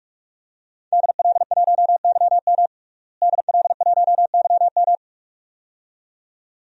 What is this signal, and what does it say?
Morse code, heard as a keyed tone, about 43 wpm, 700 Hz. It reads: DL1YM DL1YM
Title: DXCC-100 Call Signs - 5 Characters - Repeated 2x - 45wpm